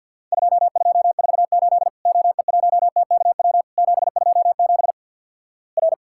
Morse code, heard as a keyed tone, about 50 wpm, 700 Hz. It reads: 2248 OE1TKW 727 R
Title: Worked All Europe DX Contest - QTC Exchange - 50wpm